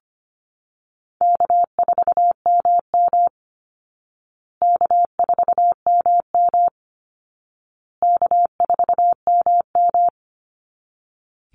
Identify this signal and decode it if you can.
Morse code, heard as a keyed tone, about 25 wpm, 700 Hz. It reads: K4MM K4MM K4MM
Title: Easy Call Signs Repeated 3x - 25wpm